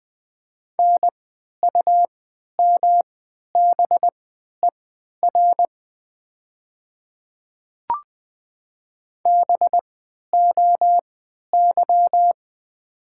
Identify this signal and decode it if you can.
Morse code, heard as a keyed tone, about 20 wpm, spaced as for 11 wpm, 700 Hz. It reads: NUMBER BOY
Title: 3x Character Spacing - Top 200 Words - 20wpm